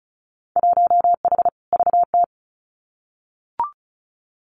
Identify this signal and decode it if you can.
Morse code, heard as a keyed tone, about 36 wpm, 700 Hz. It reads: 1H VT
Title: Winter Field Day Exchanges 35wpm